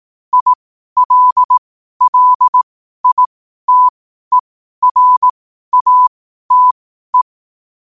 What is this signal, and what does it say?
Morse code, heard as a keyed tone, about 18 wpm, spaced as for 12 wpm, 1000 Hz. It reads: ILLITERATE
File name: qotdfiles / 12wpm